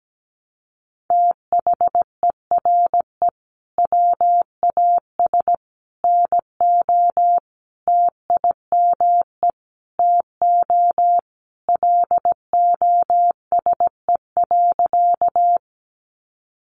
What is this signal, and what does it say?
Morse code, heard as a keyed tone, about 17 wpm, 700 Hz. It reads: THERE WAS NO TIME TO LOSE.
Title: Sentences from Top 1100 Words 17wpm